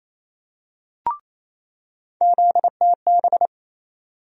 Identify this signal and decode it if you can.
Morse code, heard as a keyed tone, about 28 wpm, 700 Hz. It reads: ZTB
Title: USA Call Sign Postfixes - 3 Letters 28wpm